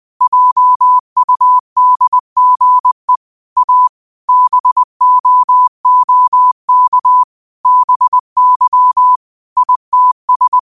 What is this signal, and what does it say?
Morse code, heard as a keyed tone, about 20 wpm, 1000 Hz. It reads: JUDGE A BOOK BY ITS